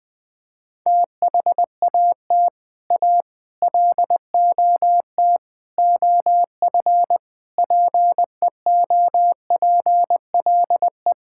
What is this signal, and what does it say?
Morse code, heard as a keyed tone, about 20 wpm, 700 Hz. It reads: THAT A LOT OF PEOPLE
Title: Build A Phrase - 5-Word Phrases - 20wpm